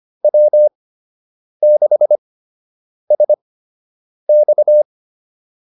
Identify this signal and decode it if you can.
Morse code, heard as a keyed tone, about 25 wpm, spaced as for 8 wpm, 600 Hz. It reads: W6SX